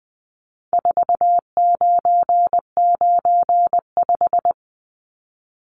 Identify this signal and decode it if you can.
Morse code, heard as a keyed tone, about 19 wpm, 700 Hz. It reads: 4995
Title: Numbers - 4-Digit 20wpm